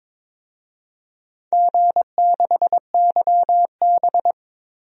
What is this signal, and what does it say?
Morse code, heard as a keyed tone, about 22 wpm, 700 Hz. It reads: G6YB
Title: DXCC-100 Call Signs - 4 Characters - 22wpm